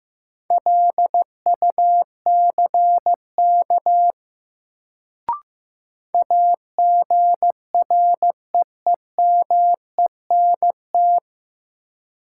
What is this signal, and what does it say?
Morse code, heard as a keyed tone, about 15 wpm, 700 Hz. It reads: LUCK AGREEMENT
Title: Sentences from Top 1500 Words - New Word Drill - 15wpm